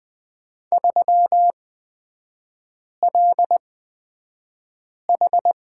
Morse code, heard as a keyed tone, about 20 wpm, spaced as for 5 wpm, 690 Hz. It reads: 3LH
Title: 229 Lesson 37e - Letter x - Three Characters 20wpm